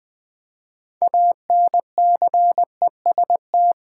Morse code, heard as a keyed tone, about 20 wpm, 700 Hz. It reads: ANCEST
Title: Build A Word - 3 Letters At A Time - 8-Letter Words - 20wpm